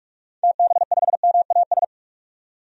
Morse code, heard as a keyed tone, about 47 wpm, 700 Hz. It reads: TB5MAS
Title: DXCC-100 Call Signs - 6 Characters - 45wpm